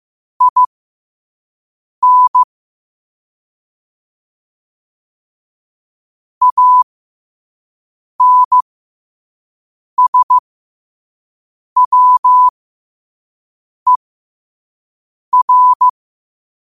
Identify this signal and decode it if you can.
Morse code, heard as a keyed tone, about 15 wpm, spaced as for 5 wpm, 1000 Hz. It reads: IN ANSWER